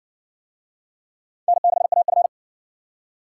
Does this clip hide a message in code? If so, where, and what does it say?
Yes: Morse code from 1.5 to 2.3 s, heard as a keyed tone, about 55 wpm, 700 Hz. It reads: N6AF